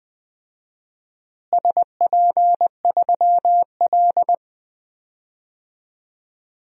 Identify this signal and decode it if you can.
Morse code, heard as a keyed tone, about 20 wpm, 700 Hz. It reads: SP3L
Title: DXCC-100 Call Signs - 4 Characters - Repeated 2x - 20wpm